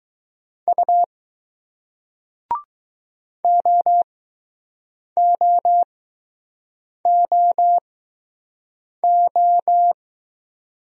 Morse code, heard as a keyed tone, about 20 wpm, spaced as for 6 wpm, 700 Hz. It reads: U OOOO